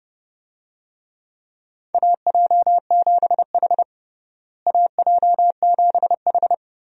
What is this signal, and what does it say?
Morse code, heard as a keyed tone, about 30 wpm, 700 Hz. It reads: AJ7H AJ7H
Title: USA Call Signs - 4 Characters - Repeated 2x - 30wpm